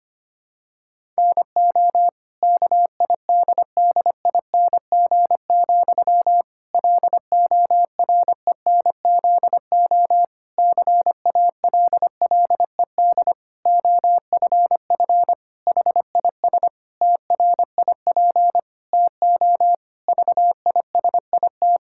Morse code, heard as a keyed tone, about 25 wpm, 700 Hz. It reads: NO KIDDING, LORENZO CALLED OFF HIS TRIP TO VISIT
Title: Pangram Challenge v2 - 25wpm